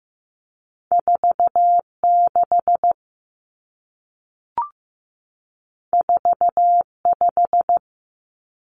Morse code, heard as a keyed tone, about 15 wpm, 700 Hz. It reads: 4 6 4 5
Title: Common Cut Numbers - 2-Digit 15wpm